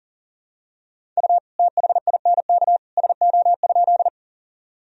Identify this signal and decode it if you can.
Morse code, heard as a keyed tone, about 38 wpm, 700 Hz. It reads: U THINK SO?
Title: Sentences from Top 200 Words and Abbreviations 40wpm